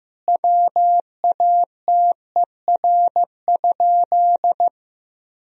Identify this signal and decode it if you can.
Morse code, heard as a keyed tone, about 15 wpm, 700 Hz. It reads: WATER?
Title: Sentences from Top 1000 Words - Review Part 3 - 15wpm